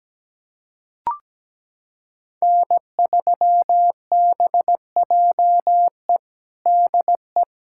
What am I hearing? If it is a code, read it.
Morse code, heard as a keyed tone, about 17 wpm, 700 Hz. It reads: N3BJE DE
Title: USA Callsign DE Callsign - 5 Characters - 17wpm